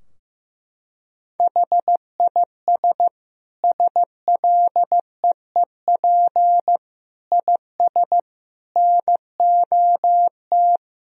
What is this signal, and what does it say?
Morse code, heard as a keyed tone, about 16 wpm, 700 Hz. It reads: HIS SLEEP IS NOT
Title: Sentences from Top 1000 Words - Review Part 4 - 15wpm